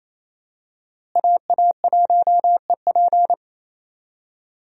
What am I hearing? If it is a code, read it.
Morse code, heard as a keyed tone, about 28 wpm, 700 Hz. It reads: AA1EP
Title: USA Call Signs - 5 Characters - Repeated 2x - 28wpm